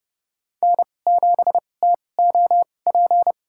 Morse code, heard as a keyed tone, about 30 wpm, spaced as for 22 wpm, 700 Hz. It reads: N7TOP
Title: USA Call Signs - 2x Character Spacing - 30wpm